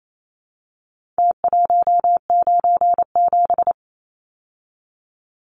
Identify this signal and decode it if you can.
Morse code, heard as a keyed tone, about 28 wpm, 700 Hz. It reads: T197
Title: Numbers - Contest Style - 4-Digit 28wpm